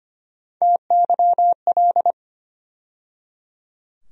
Morse code, heard as a keyed tone, about 25 wpm, 700 Hz. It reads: TYL
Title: USA Call Sign Postfixes - 3 Letters 25wpm